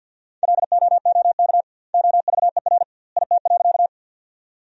Morse code, heard as a keyed tone, about 50 wpm, 700 Hz. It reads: LOOK OVER IT.
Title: 2x Word-Spacing - Sentences from Top 100 Words - 50wpm